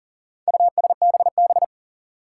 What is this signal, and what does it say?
Morse code, heard as a keyed tone, about 40 wpm, 700 Hz. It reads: USBB
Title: Four Challenging Letter-Number Combinations 40wpm